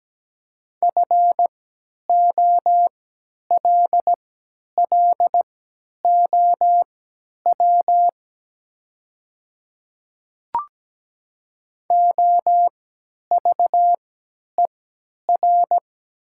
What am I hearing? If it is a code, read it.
Morse code, heard as a keyed tone, about 17 wpm, spaced as for 10 wpm, 700 Hz. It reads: FOLLOW OVER
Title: 3x Character Spacing - Top 300 Words - 17wpm